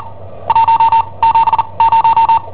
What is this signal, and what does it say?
Morse code, heard as a keyed tone, about 41 wpm, spaced as for 27 wpm, 910 Hz. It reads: 170
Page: Cassetta esercizi preparata da I7OHP per IW5ELP